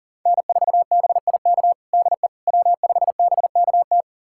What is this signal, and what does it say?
Morse code, heard as a keyed tone, about 40 wpm, 700 Hz. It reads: N4BIK DE W5BKT
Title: USA Callsign DE Callsign - 5 Characters - 40wpm